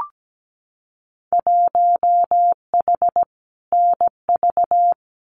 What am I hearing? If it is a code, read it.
Morse code, heard as a keyed tone, about 17 wpm, 700 Hz. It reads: 1H NV
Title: Winter Field Day Exchanges 17wpm